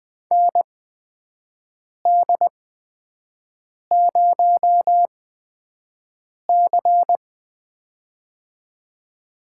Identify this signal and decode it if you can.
Morse code, heard as a keyed tone, about 20 wpm, spaced as for 5 wpm, 700 Hz. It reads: ND0C
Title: USA Call Signs - 4 Characters - 8x Character Spacing - 20wpm